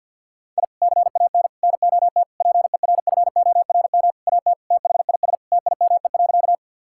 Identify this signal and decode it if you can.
Morse code, heard as a keyed tone, about 52 wpm, 700 Hz. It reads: I CAN NOT PERFORM AT THIS TIME.
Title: Sentences from Top 1000 Words - Review Part 5 - 50wpm